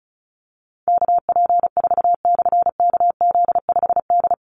Morse code, heard as a keyed tone, about 34 wpm, 700 Hz. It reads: KP4/KZ5D
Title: Call Signs Repeated 3x - 35wpm